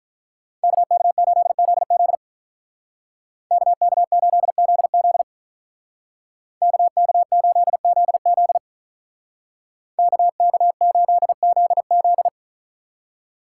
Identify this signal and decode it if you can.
Morse code, heard as a keyed tone, about 46 wpm, 700 Hz. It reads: KK8ZZ KK8ZZ KK8ZZ KK8ZZ